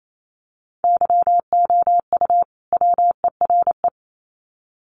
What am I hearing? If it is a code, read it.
Morse code, heard as a keyed tone, about 28 wpm, 700 Hz. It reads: YOU WERE
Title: Sets of 2 Words 28wpm